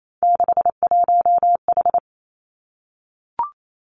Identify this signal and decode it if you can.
Morse code, heard as a keyed tone, about 28 wpm, 700 Hz. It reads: 61H
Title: Three Challenging Letter-Number Combinations 28wpm